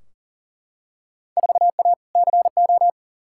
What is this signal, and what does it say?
Morse code, heard as a keyed tone, about 40 wpm, 700 Hz. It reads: 4A CO
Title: Field Day Exchanges - 40wpm